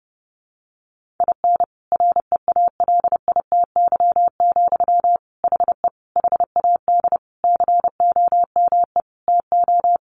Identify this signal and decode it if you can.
Morse code, heard as a keyed tone, about 30 wpm, 700 Hz. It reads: IN REALITY, HE HAD COME TO
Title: Sentences from Top 1500 Words 30wpm